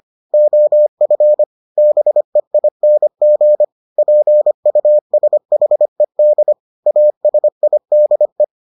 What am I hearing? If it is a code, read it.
Morse code, heard as a keyed tone, about 25 wpm, 600 Hz. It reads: OF BEING PUSHED ASIDE